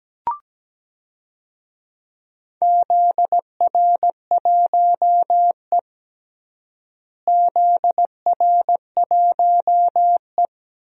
Morse code, heard as a keyed tone, about 17 wpm, 700 Hz. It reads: ZR1E ZR1E
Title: DXCC-100 Call Signs - 4 Characters - Repeated 3x - 17wpm